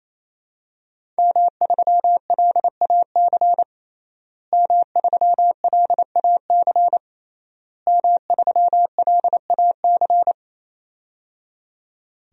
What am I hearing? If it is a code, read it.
Morse code, heard as a keyed tone, about 28 wpm, 700 Hz. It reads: M3LAC M3LAC M3LAC
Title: DXCC-100 Call Signs - 5 Characters - Repeated 3x - 28wpm